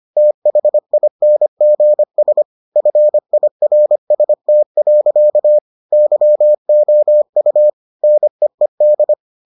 Morse code, heard as a keyed tone, about 25 wpm, 600 Hz. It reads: THINGS FIRST. YOU NEED